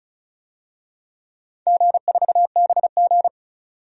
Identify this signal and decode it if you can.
Morse code, heard as a keyed tone, about 36 wpm, 700 Hz. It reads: G4BG
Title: DXCC-100 Call Signs - 4 Characters - 35wpm